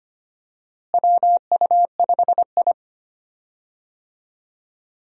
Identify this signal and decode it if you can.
Morse code, heard as a keyed tone, about 25 wpm, 700 Hz. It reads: WU5I
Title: USA Call Signs - 4 Characters - Repeated 2x - 25wpm